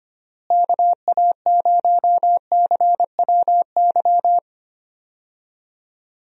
Morse code, heard as a keyed tone, about 25 wpm, 700 Hz. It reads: KA0CWY
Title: USA Call Signs - 6 Characters - Repeated 3x - 25wpm